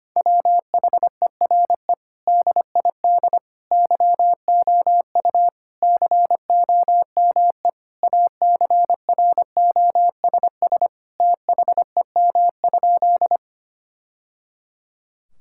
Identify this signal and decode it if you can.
Morse code, heard as a keyed tone, about 25 wpm, 700 Hz. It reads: WHERE DID YOU COME ACROSS THEM?
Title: Sentences from Top 1000 Words - Review Part 5 - 25wpm